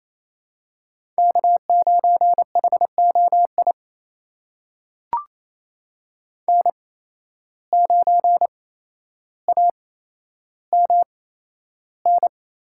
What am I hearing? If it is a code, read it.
Morse code, heard as a keyed tone, about 28 wpm, spaced as for 12 wpm, 700 Hz. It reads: K9HOI N9AMN
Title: USA Call Signs - 8x Character Spacing - 28wpm